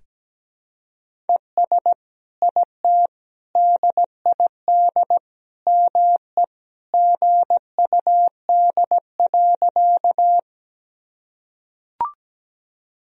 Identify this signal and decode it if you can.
Morse code, heard as a keyed tone, about 17 wpm, 700 Hz. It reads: ES IT DID ME GUD.
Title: Sentences from Top 200 Words and Abbreviations 17wpm